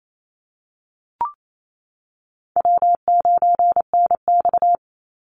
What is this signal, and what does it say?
Morse code, heard as a keyed tone, about 28 wpm, 700 Hz. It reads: W9NX
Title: Easy Call Signs - 28wpm